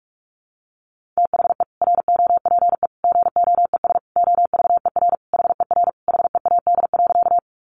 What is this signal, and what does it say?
Morse code, heard as a keyed tone, about 44 wpm, 700 Hz. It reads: THE ROPE GOES OVER HER HEAD.
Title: Easy Phrases 45wpm